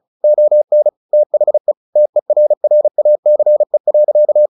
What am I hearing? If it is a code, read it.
Morse code, heard as a keyed tone, about 34 wpm, 600 Hz. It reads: ON THE TERRACE.
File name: the old man and the sea / the old man and the sea 1 35wpm